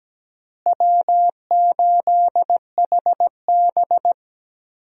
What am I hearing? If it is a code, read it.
Morse code, heard as a keyed tone, about 17 wpm, 700 Hz. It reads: W8HB
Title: USA Call Signs - 4 Characters - Repeated 3x - 17wpm